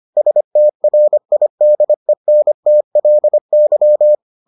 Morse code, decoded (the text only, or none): STRIDENTLY